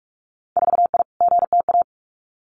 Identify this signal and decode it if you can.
Morse code, heard as a keyed tone, about 46 wpm, 700 Hz. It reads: 4I GTA
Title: Winter Field Day Exchanges Encore 45wpm